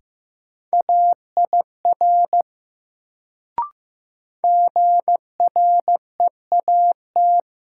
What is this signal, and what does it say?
Morse code, heard as a keyed tone, about 15 wpm, 700 Hz. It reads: AIR GREAT
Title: Sentences from Top 200 Words - New Word Drill - 15wpm